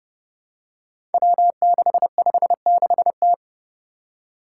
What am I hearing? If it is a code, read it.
Morse code, heard as a keyed tone, about 30 wpm, 700 Hz. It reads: W656T
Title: Five Challenging Letter-Number Combinations 30wpm